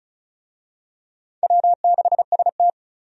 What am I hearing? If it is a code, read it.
Morse code, heard as a keyed tone, about 36 wpm, 700 Hz. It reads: W6ST
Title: USA Call Signs - 4 Characters - 35wpm